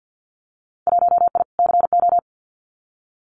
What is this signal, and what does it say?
Morse code, heard as a keyed tone, about 50 wpm, 700 Hz. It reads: 1I CO
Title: Winter Field Day Exchanges Encore 50wpm